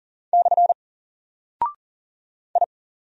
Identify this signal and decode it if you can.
Morse code, heard as a keyed tone, about 40 wpm, 700 Hz. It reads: / I
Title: Single Character (Letter:Number:Punctuation:Prosign) Alt 40wpm